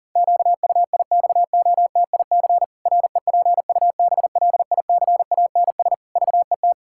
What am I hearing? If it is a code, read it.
Morse code, heard as a keyed tone, about 40 wpm, 700 Hz. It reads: QUIXOTIC REPUBLICANS VET
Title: Pangram Challenge v2 - 40wpm